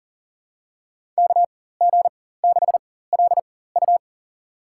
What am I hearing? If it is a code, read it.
Morse code, heard as a keyed tone, about 37 wpm, spaced as for 18 wpm, 700 Hz. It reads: KG6LU
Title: USA Call Signs - 5 Characters - 4x Character Spacing - 40wpm